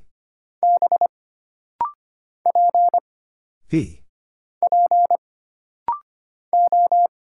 Morse code, heard as a keyed tone, about 25 wpm, 700 Hz. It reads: B P P O
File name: Single Character (Letter:Number:Punctuation:Prosign) - Rapid-Fire - 25wpm